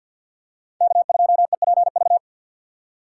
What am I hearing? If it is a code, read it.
Morse code, heard as a keyed tone, about 52 wpm, 700 Hz. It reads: K2EPV
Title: USA Call Signs - 5 Characters - 4x Character Spacing - 50wpm